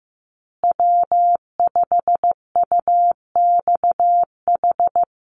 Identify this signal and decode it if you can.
Morse code, heard as a keyed tone, about 15 wpm, 700 Hz. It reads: W5UXH
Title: Call Signs Repeated 2x - 15wpm